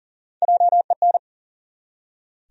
Morse code, heard as a keyed tone, about 40 wpm, 700 Hz. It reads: JEN
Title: USA Common Names - Max 3 Letters - 40wpm